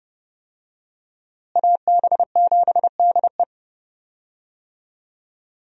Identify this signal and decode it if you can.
Morse code, heard as a keyed tone, about 32 wpm, 700 Hz. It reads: AB7DE
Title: USA Call Signs - 5 Characters - Repeated 3x - 30wpm